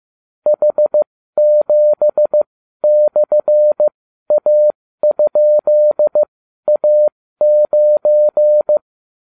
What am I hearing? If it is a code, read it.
Morse code, heard as a keyed tone, about 15 wpm, spaced as for 12 wpm, 600 Hz. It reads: H7/A?A9